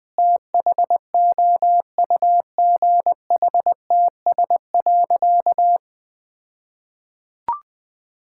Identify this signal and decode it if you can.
Morse code, heard as a keyed tone, about 20 wpm, 700 Hz. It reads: THOUGHTS.
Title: All-Age Appropriate Jokes - 20wpm